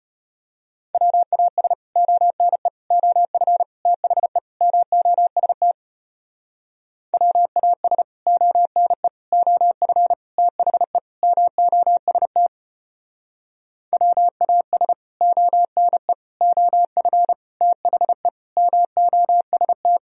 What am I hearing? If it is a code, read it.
Morse code, heard as a keyed tone, about 32 wpm, 700 Hz. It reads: WAS ONE OF THE MOST WAS ONE OF THE MOST WAS ONE OF THE MOST